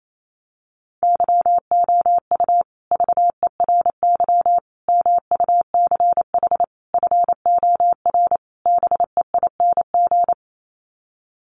Morse code, heard as a keyed tone, about 28 wpm, 700 Hz. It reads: YOU VERY MUCH FOR BEING
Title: Sets of 5 Words Encore 28wpm